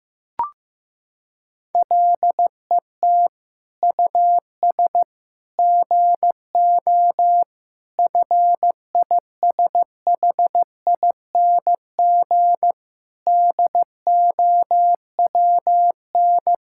Morse code, heard as a keyed tone, about 15 wpm, 700 Hz. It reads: LET US GO FISHING DOWN
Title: Sentences from Top 1000 Words - Review Part 5 - 15wpm